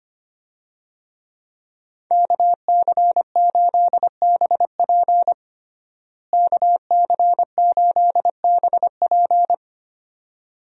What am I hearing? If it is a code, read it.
Morse code, heard as a keyed tone, about 25 wpm, 700 Hz. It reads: KC8BP KC8BP
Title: USA Call Signs - 5 Characters - Repeated 2x - 25wpm